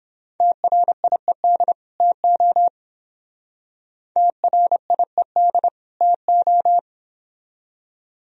Morse code, heard as a keyed tone, about 26 wpm, 700 Hz. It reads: TRIED TO TRIED TO
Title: Speed-Racing - Sets of 2 Words 22wpm